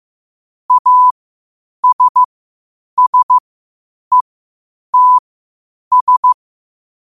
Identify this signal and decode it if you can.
Morse code, heard as a keyed tone, about 15 wpm, spaced as for 8 wpm, 1000 Hz. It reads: ASSETS
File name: qotdfiles / 8wpm